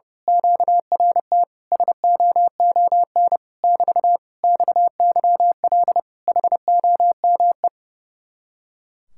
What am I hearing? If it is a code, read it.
Morse code, heard as a keyed tone, about 28 wpm, 700 Hz. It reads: QRT SOON <BT> XYL HOME
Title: Speed-Racing Common QSO Phrases Encore - 30wpm